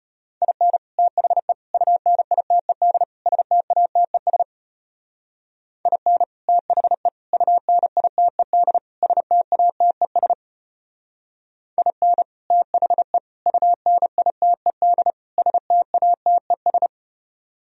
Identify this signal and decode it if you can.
Morse code, heard as a keyed tone, about 33 wpm, 700 Hz. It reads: IN THE UNITED STATES IN THE UNITED STATES IN THE UNITED STATES